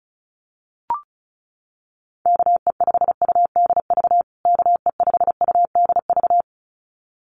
Morse code, heard as a keyed tone, about 34 wpm, 700 Hz. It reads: KE5UDV KE5UDV
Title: USA Call Signs Repeated 2x 35wpm